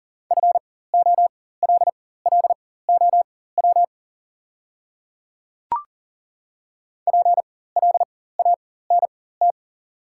Morse code, heard as a keyed tone, about 40 wpm, spaced as for 19 wpm, 700 Hz. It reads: FOLLOW PLANT
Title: 4x Character Spacing - Top 300 Words - 40wpm